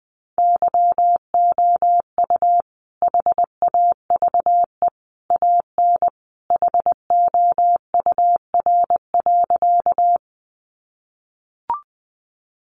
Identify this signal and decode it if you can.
Morse code, heard as a keyed tone, about 20 wpm, 700 Hz. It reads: YOU HAVE AN HOUR.
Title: Sentences from Top 400 Words 20wpm